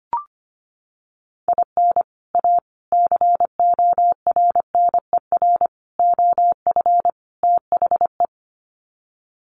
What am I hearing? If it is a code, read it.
Morse code, heard as a keyed tone, about 25 wpm, 700 Hz. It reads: IN A CORNER OF THE
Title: Sets of 5 Words Encore2 25wpm